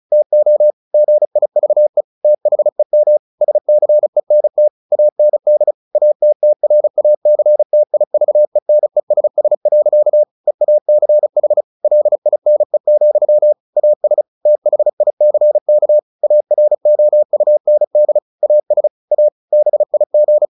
Morse code, heard as a keyed tone, about 34 wpm, 600 Hz. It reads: TO GIVE THEM SCENT AND ATTRACTIVENESS. EACH LINE, AS THICK AROUND AS A BIG
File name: the old man and the sea / the old man and the sea 1 35wpm